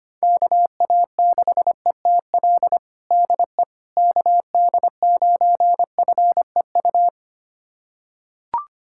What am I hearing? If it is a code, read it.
Morse code, heard as a keyed tone, about 25 wpm, 700 Hz. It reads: KA6ETL DE KD9FEU
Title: USA Callsign DE Callsign - 6 Characters - 25wpm